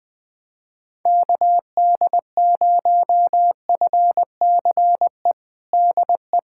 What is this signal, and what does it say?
Morse code, heard as a keyed tone, about 21 wpm, 700 Hz. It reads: KD0FCE DE
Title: USA Callsign DE Callsign 20wpm